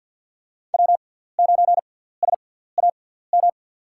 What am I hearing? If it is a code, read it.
Morse code, heard as a keyed tone, about 49 wpm, spaced as for 17 wpm, 700 Hz. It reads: W9SAM
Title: USA Call Signs - 5 Characters - 6x Character Spacing - 50wpm